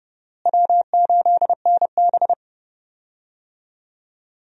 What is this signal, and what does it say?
Morse code, heard as a keyed tone, about 30 wpm, 700 Hz. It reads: W8NB
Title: USA Call Signs - 4 Characters - Repeated 2x - 30wpm